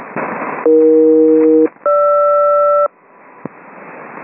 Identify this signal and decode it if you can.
Aircraft SELCAL, heard as two pairs of tones. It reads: AE-GR
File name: uploader / 27nov 6715 hunter21